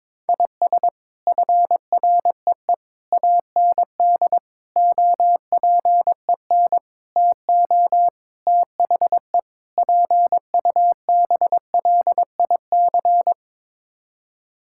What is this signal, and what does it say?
Morse code, heard as a keyed tone, about 22 wpm, 700 Hz. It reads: IS FREE AND OPEN TO THE PUBLIC
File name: practice / 22wpm / Sets of 7 Words - 22wpm